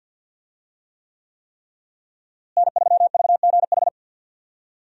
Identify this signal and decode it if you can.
Morse code, heard as a keyed tone, about 52 wpm, 700 Hz. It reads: N3VGH
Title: USA Call Signs - 5 Characters - Repeated 3x - 50wpm